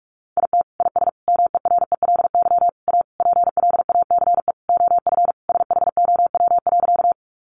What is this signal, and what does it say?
Morse code, heard as a keyed tone, about 44 wpm, 700 Hz. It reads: IT IS MERELY A PLACE OF SHOW.
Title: Sentences from Top 1400 Words 45wpm